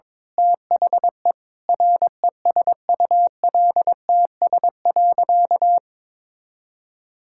Morse code, heard as a keyed tone, about 22 wpm, 700 Hz. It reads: THE RESULTS.
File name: Sentences with 7-Word Phrases - 22wpm